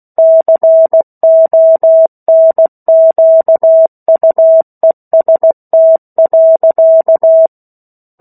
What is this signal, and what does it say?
Morse code, heard as a keyed tone, about 16 wpm, 650 Hz. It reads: CONQUEST.
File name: 03-Chapter-03 16 WPM